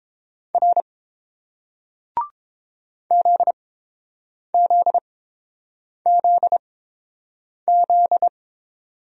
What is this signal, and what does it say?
Morse code, heard as a keyed tone, about 30 wpm, spaced as for 7 wpm, 700 Hz. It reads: R ZZZZ